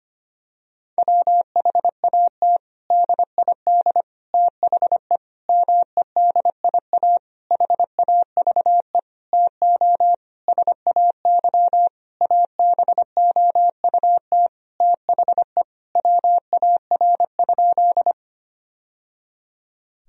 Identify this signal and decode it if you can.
Morse code, heard as a keyed tone, about 25 wpm, 700 Hz. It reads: WHAT DID THE MEDIA HAVE TO SAY ABOUT THE WAR?
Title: Sentences from Top 1000 Words - Review Part 4 - 25wpm